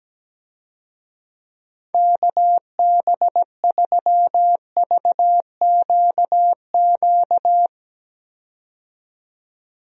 Morse code, heard as a keyed tone, about 17 wpm, 700 Hz. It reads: KB3VQQ
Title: USA Call Signs - 6 Characters - 17wpm